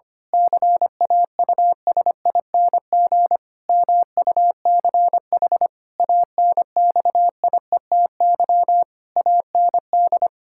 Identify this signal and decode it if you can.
Morse code, heard as a keyed tone, about 25 wpm, 700 Hz. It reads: CAUSING MUCH ANXIETY AND
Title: Pangram Challenge v2 - 25wpm